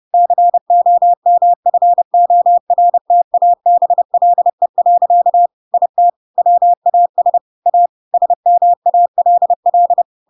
Morse code, heard as a keyed tone, about 30 wpm, 700 Hz. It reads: COMFORTABLE. IT WAS A SMALL
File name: Pinocchio / Pinocchio ch3 30 30 ik1hge